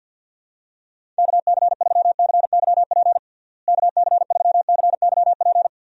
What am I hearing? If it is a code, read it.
Morse code, heard as a keyed tone, about 46 wpm, 700 Hz. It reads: KC3CYP KC3CYP
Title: USA Call Signs - 6 Characters - Repeated 3x - 50wpm